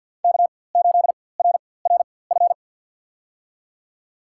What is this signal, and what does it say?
Morse code, heard as a keyed tone, about 51 wpm, spaced as for 23 wpm, 700 Hz. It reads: K8RRF
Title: USA Call Signs - 5 Characters - 4x Character Spacing - 50wpm